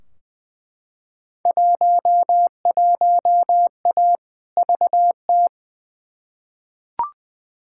Morse code, heard as a keyed tone, about 20 wpm, 700 Hz. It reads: 11A VT
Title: Field Day Exchanges Encore - 20wpm